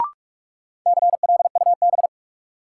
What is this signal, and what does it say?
Morse code, heard as a keyed tone, about 42 wpm, 700 Hz. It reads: CLUB
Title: Sentences from Top 1200 Words - New Word Drill Alt - 45wpm